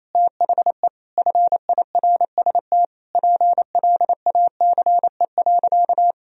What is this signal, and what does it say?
Morse code, heard as a keyed tone, about 28 wpm, 700 Hz. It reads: THE FIRST PLACE.
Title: Sentences from Top 1000 Words - Review Part 1 - 28wpm